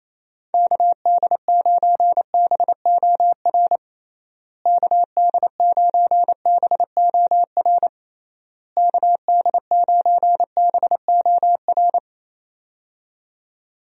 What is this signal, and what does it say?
Morse code, heard as a keyed tone, about 28 wpm, 700 Hz. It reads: KD9BOR KD9BOR KD9BOR
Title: USA Call Signs - 6 Characters - Repeated 3x - 28wpm